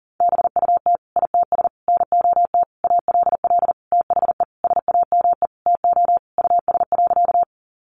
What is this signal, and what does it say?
Morse code, heard as a keyed tone, about 40 wpm, 700 Hz. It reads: BUT ITS NOT ALL THE SAME TO US.
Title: Sentences from Top 600 Words 40wpm